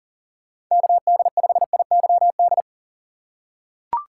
Morse code, heard as a keyed tone, about 37 wpm, 700 Hz. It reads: KD5IYD
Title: USA Call Signs - 6 Characters - 4x Character Spacing - 40wpm